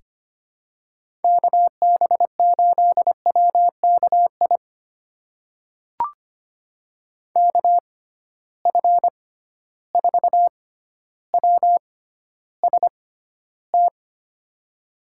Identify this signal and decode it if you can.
Morse code, heard as a keyed tone, about 25 wpm, spaced as for 13 wpm, 700 Hz. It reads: KB8WKI KF4WST
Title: USA Call Signs - 6 Characters - 6x Character Spacing - 25wpm